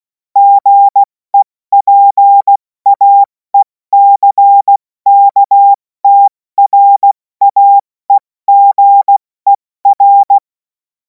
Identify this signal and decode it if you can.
Morse code, heard as a keyed tone, about 16 wpm, 800 Hz. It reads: GEPAECKTRAEGER
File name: Morsekurs / Witze / Witz 31. März 2026 70 F80